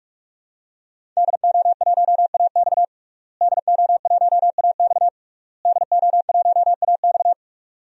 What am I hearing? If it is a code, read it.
Morse code, heard as a keyed tone, about 46 wpm, 700 Hz. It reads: DO1AX DO1AX DO1AX
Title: DXCC-100 Call Signs - 5 Characters - Repeated 3x - 45wpm